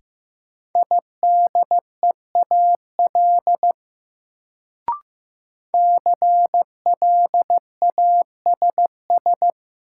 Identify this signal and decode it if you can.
Morse code, heard as a keyed tone, about 15 wpm, 700 Hz. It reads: IDEAL CLASS
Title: Sentences from Top 1400 Words - New Word Drill Alt - 15wpm